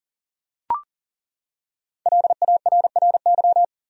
Morse code, heard as a keyed tone, about 40 wpm, 700 Hz. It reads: LARRY
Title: USA Common Names - 40wpm